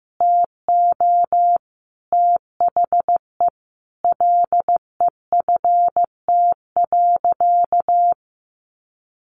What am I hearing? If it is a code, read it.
Morse code, heard as a keyed tone, about 15 wpm, 700 Hz. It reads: TO THE LEFT.
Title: Sentences from Top 300 Words 15wpm